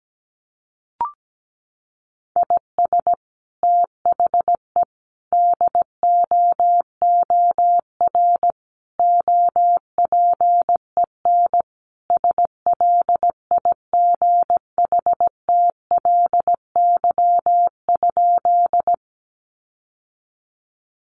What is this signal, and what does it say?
Morse code, heard as a keyed tone, about 17 wpm, 700 Hz. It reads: IS THE DOOR OPEN SLIGHTLY?
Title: Sentences from Top 1300 Words 17wpm